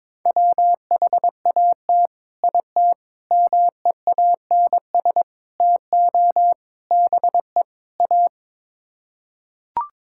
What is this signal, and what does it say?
Morse code, heard as a keyed tone, about 22 wpm, 700 Hz. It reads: WHAT IT MEANS TO BE A
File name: practice / 22wpm / Sets of 6 Words - 22wpm